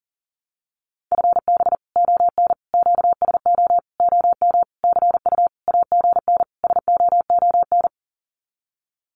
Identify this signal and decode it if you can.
Morse code, heard as a keyed tone, about 40 wpm, 700 Hz. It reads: FB ON QSO OM CU AGN SOON
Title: Common QSO Phrases 40wpm